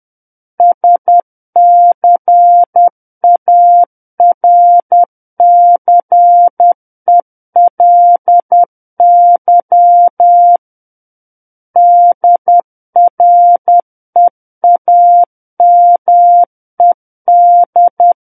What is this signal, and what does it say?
Morse code, heard as a keyed tone, about 10 wpm, 700 Hz. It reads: SCARCELY DREAMED
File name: Book One / 01-Chapter 1 - 10 WPM